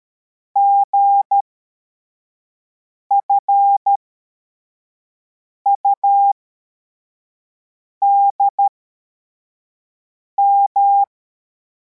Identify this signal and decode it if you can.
Morse code, heard as a keyed tone, about 13 wpm, spaced as for 4 wpm, 800 Hz. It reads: GFUDM
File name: formation / cw / Text73